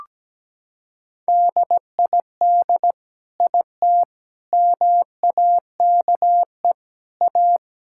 Morse code, heard as a keyed tone, about 17 wpm, 700 Hz. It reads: DID IT MAKE A
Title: Sentences from Top 1000 Words - Review Part 5 - 17wpm